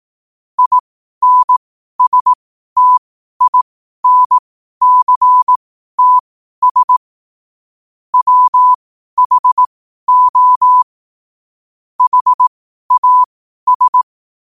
Morse code, heard as a keyed tone, about 18 wpm, spaced as for 12 wpm, 1000 Hz. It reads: INSTINCTS WHO HAS
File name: qotdfiles / 12wpm